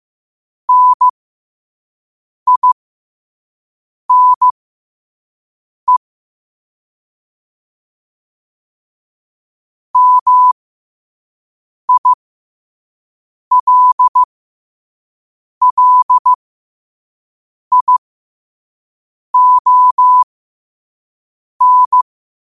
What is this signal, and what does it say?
Morse code, heard as a keyed tone, about 15 wpm, spaced as for 5 wpm, 1000 Hz. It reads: NINE MILLION
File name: qotdfiles / 5wpm